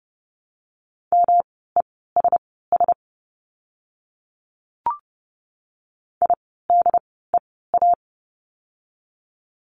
Morse code, heard as a keyed tone, about 30 wpm, spaced as for 17 wpm, 700 Hz. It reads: MESS IDEA
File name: course / 30:10wpm / 108 Lesson 18f - Letter m - Words 30wpm